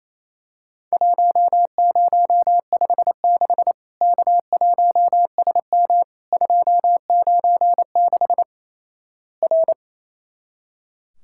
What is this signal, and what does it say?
Morse code, heard as a keyed tone, about 28 wpm, 700 Hz. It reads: 1056 K1SM 296 R
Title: Worked All Europe DX Contest - QTC Exchange - 28wpm